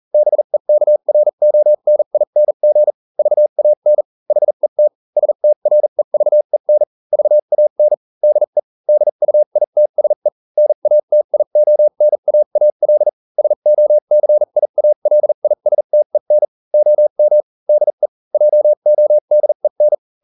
Morse code, heard as a keyed tone, about 40 wpm, spaced as for 35 wpm, 600 Hz. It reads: BEKRONING VAN HET STREVEN VAN DE DUITSE NATIONAAL SOCIALISTEN OM DE JODEN